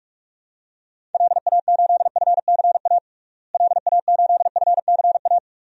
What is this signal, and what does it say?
Morse code, heard as a keyed tone, about 46 wpm, 700 Hz. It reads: LA8FCA LA8FCA
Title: DXCC-100 Call Signs - 6 Characters - Repeated 2x - 45wpm